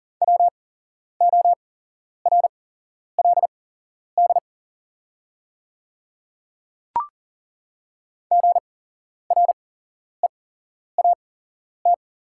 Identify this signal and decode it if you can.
Morse code, heard as a keyed tone, about 38 wpm, spaced as for 11 wpm, 700 Hz. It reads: WORLD GREAT
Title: 8x Character Spacing - Top 200 Words - 40wpm